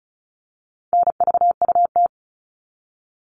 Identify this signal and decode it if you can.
Morse code, heard as a keyed tone, about 34 wpm, 700 Hz. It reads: NVUT
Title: All Cut Numbers - 4-Digit 35wpm